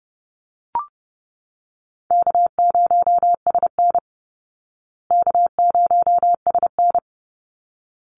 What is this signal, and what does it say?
Morse code, heard as a keyed tone, about 28 wpm, 700 Hz. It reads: K0SN K0SN
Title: Easy Call Signs Repeated 2x - 30wpm